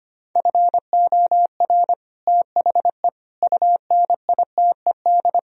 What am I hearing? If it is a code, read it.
Morse code, heard as a keyed tone, about 32 wpm, 700 Hz. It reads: FOR THE UNITED